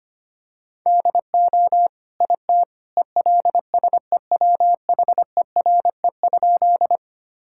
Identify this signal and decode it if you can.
Morse code, heard as a keyed tone, about 25 wpm, 700 Hz. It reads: DO IT ELSEWHERE?
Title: Sentences from Top 1600 Words 25wpm